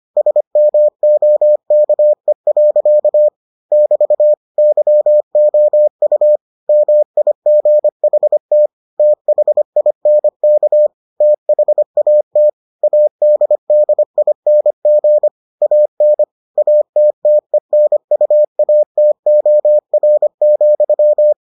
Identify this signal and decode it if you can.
Morse code, heard as a keyed tone, about 25 wpm, 600 Hz. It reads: SMOKE. <BT> YOU MIGHT THINK THAT ADDING AN ATTENUATOR,